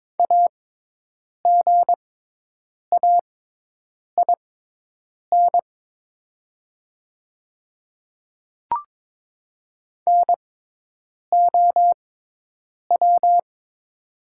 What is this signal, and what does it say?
Morse code, heard as a keyed tone, about 22 wpm, spaced as for 8 wpm, 700 Hz. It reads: AGAIN NOW
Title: 6x Character Spacing - Top 200 Words - 22wpm